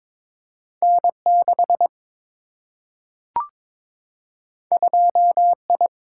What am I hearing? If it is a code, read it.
Morse code, heard as a keyed tone, about 21 wpm, 700 Hz. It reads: N 6 2 I
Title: Single-Letter Single-Number 22wpm